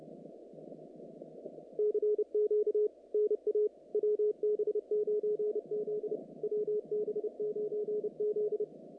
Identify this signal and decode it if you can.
Morse code, heard as a keyed tone, about 30 wpm, 440 Hz. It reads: CQ NA WB9Z WB9Z
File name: contests / naqp-cw-k5tr-op-k5pi- -2200